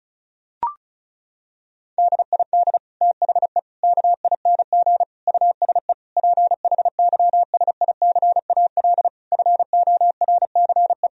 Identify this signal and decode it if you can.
Morse code, heard as a keyed tone, about 34 wpm, 700 Hz. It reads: DID THE KING USE PHYSICAL FORCE
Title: Sentences from Top 1000 Words - Review Part 4 - 35wpm